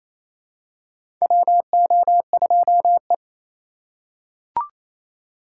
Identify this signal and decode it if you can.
Morse code, heard as a keyed tone, about 28 wpm, 700 Hz. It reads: WO2E
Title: K1USN SST QSO Practice - 28wpm